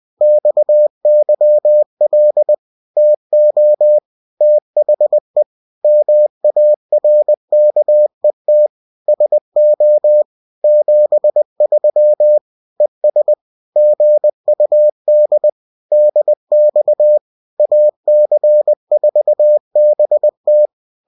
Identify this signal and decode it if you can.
Morse code, heard as a keyed tone, about 20 wpm, 600 Hz. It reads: XYL TO THE MARKET SO 73 ES GUD DX AC4BT